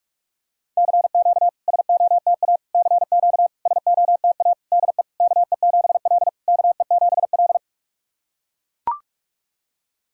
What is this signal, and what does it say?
Morse code, heard as a keyed tone, about 41 wpm, 690 Hz. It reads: CQ SOTA CQ SOTA DE KE7L KE7L
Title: SOTA QSO Practice 45wpm